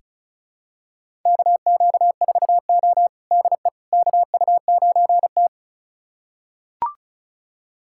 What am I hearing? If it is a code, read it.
Morse code, heard as a keyed tone, about 36 wpm, 700 Hz. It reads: KQ4O DE KU9T
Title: USA Callsign DE Callsign - 4 Characters - 35wpm